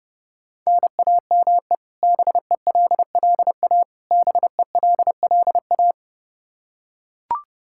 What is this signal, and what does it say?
Morse code, heard as a keyed tone, about 32 wpm, 700 Hz. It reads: NAME BELLA BELLA
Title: Practice QSO - Exchange 1 - Beginning - 30wpm